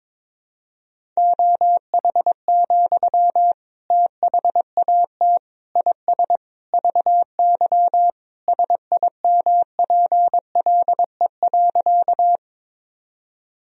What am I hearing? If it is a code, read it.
Morse code, heard as a keyed tone, about 22 wpm, 700 Hz. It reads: OH, THAT IS VY SIMPLE.
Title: Sentences from Top 500 Words and Abbreviations 22wpm